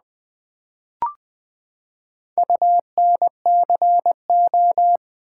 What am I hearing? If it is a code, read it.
Morse code, heard as a keyed tone, about 20 wpm, 700 Hz. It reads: UNCO
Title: Build A Word - 4 Letters At A Time - 11-Letter Words - 20wpm